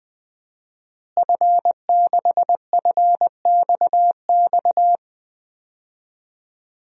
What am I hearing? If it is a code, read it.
Morse code, heard as a keyed tone, about 20 wpm, 700 Hz. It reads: F6FXX
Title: DXCC-100 Call Signs - 5 Characters - Repeated 3x - 20wpm